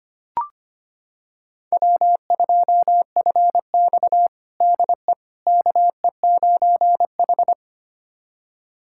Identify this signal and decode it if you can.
Morse code, heard as a keyed tone, about 25 wpm, 700 Hz. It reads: W2FX DE KE9H
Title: USA Callsign DE Callsign - 4 Characters - 25wpm